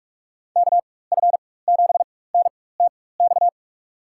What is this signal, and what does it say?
Morse code, heard as a keyed tone, about 47 wpm, spaced as for 21 wpm, 700 Hz. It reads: KF7NTX
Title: USA Call Signs - 6 Characters - 4x Character Spacing - 45wpm